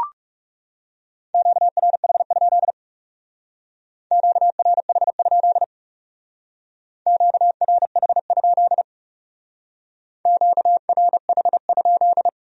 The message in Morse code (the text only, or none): QRH? QRH? QRH? QRH?